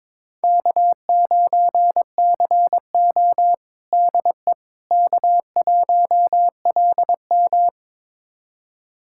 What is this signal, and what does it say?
Morse code, heard as a keyed tone, about 22 wpm, 700 Hz. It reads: K9CO DE K1LM
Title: USA Callsign DE Callsign - 4 Characters - 22wpm